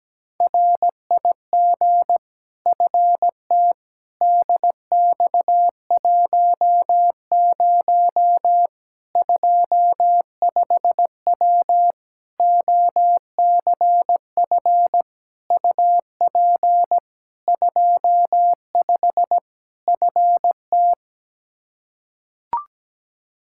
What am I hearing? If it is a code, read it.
Morse code, heard as a keyed tone, about 17 wpm, 700 Hz. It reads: RIG FT DX10 25W OCF UP 25 FT
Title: Practice QSO - Exchange 2 - Info - 17wpm